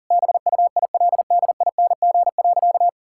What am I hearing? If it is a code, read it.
Morse code, heard as a keyed tone, about 40 wpm, 700 Hz. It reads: BUILDING.
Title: Sentences from Top 1000 Words - Review Part 6 - 40wpm